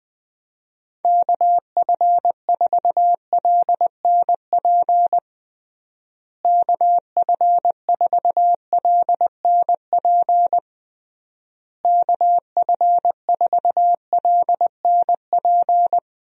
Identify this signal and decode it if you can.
Morse code, heard as a keyed tone, about 21 wpm, 700 Hz. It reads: KF4LNP KF4LNP KF4LNP
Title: USA Call Signs - 6 Characters - Repeated 3x - 20wpm